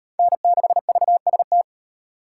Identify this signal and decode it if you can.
Morse code, heard as a keyed tone, about 34 wpm, 700 Hz. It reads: N6VST